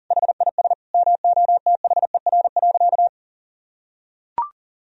Morse code, heard as a keyed tone, about 40 wpm, 700 Hz. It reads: HIS MOTHER.
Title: Sentences from Top 1000 Words - Review Part 4 - 40wpm